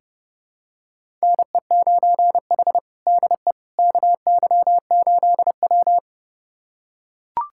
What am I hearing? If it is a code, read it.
Morse code, heard as a keyed tone, about 30 wpm, 700 Hz. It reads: NE9H DE KY8W
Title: USA Callsign DE Callsign - 4 Characters - 30wpm